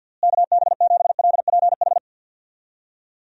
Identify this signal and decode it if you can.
Morse code, heard as a keyed tone, about 51 wpm, 700 Hz. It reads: KB7LPH
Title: USA Call Signs - 6 Characters - Repeated 3x - 50wpm